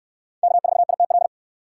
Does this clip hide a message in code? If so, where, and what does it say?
Yes: Morse code from 0.4 to 1.3 s, heard as a keyed tone, about 65 wpm, 700 Hz. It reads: K4ITL